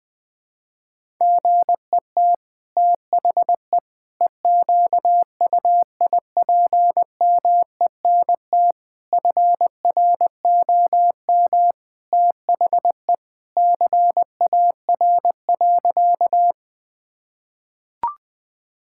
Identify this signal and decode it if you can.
Morse code, heard as a keyed tone, about 21 wpm, 700 Hz. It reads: GET THE EQUIPMENT FROM THE CAR.
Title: Sentences from Top 1000 Words - Review Part 5 - 20wpm